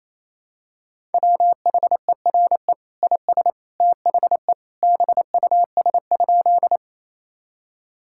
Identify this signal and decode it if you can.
Morse code, heard as a keyed tone, about 28 wpm, 700 Hz. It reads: WHERE IS THE BUS?
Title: Sentences from Top 1000 Words - Review Part 6 - 28wpm